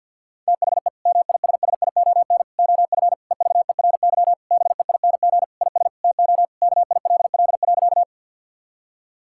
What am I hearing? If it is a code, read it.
Morse code, heard as a keyed tone, about 50 wpm, 700 Hz. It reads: THE MISSION OF EVERY BEING IS TO KILL.
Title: Sentences from Top 1600 Words 50wpm